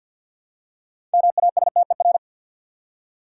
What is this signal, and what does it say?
Morse code, heard as a keyed tone, about 51 wpm, 700 Hz. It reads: MASTER
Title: Six Letter Words - 50wpm